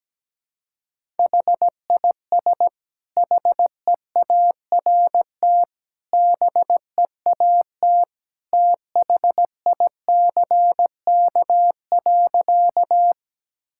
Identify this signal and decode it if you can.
Morse code, heard as a keyed tone, about 17 wpm, 700 Hz. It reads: HIS HEART BEAT THICK.
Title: Sentences from Top 1600 Words 17wpm